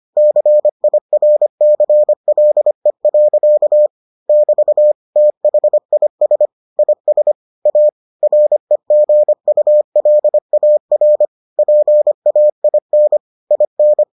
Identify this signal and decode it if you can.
Morse code, heard as a keyed tone, about 25 wpm, 600 Hz. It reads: CIRCLE. <BT> THIS IS A REGULAR PAIN IN